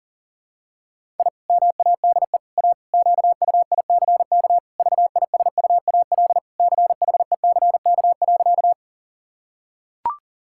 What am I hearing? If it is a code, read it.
Morse code, heard as a keyed tone, about 37 wpm, 700 Hz. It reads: I MADE A QUICK VISUAL CHECK.
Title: Sentences from Top 1000 Words - Review Part 6 - 40wpm